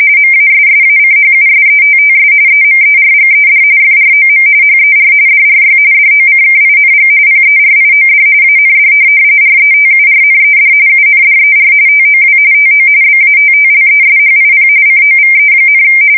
RTTY, about 45.45 baud. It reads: MASSACHUSETTS EMERGED FROM HER DRY DOCKING PERIOD AND RETURNED UNDER TOW TO BATTLESHIP COVE,